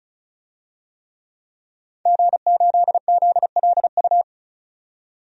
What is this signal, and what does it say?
Morse code, heard as a keyed tone, about 36 wpm, 700 Hz. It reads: G8ZLU
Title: DXCC-100 Call Signs - 5 Characters - 35wpm